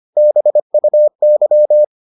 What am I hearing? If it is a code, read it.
Morse code, heard as a keyed tone, about 25 wpm, 600 Hz. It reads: BUY